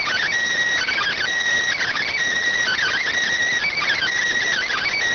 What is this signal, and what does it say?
Emergency Alert System (SAME) tones, heard as harsh buzzing bursts. It is heard from 0.8 to 4.8 s.